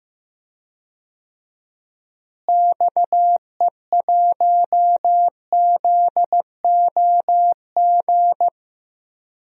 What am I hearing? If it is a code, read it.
Morse code, heard as a keyed tone, about 15 wpm, 700 Hz. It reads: XE1ZOG
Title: DXCC-100 Call Signs - 6 Characters - Repeated 3x - 15wpm